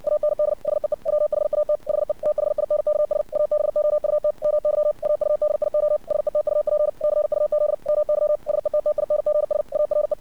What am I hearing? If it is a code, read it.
Morse code, heard as a keyed tone, about 60 wpm, 600 Hz. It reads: AND SEE WHAT HE THINKS ABOUT MY AUDIO SETUP CUZ MY SETTINGS ARE
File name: archived / CRSnet